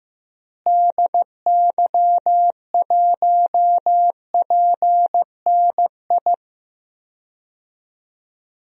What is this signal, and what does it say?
Morse code, heard as a keyed tone, about 15 wpm, 700 Hz. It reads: DY1PNI
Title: DXCC-100 Call Signs - 6 Characters - Repeated 2x - 15wpm